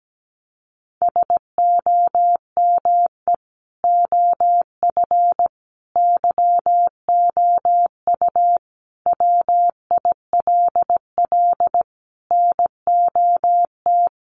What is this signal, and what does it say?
Morse code, heard as a keyed tone, about 17 wpm, 700 Hz. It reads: SOME OF YOU WILL NOT
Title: Sentences from Top 1100 Words 17wpm